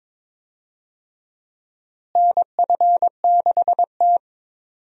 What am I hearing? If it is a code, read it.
Morse code, heard as a keyed tone, about 21 wpm, 700 Hz. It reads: NF6T
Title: USA Call Signs - 4 Characters - Repeated 2x - 22wpm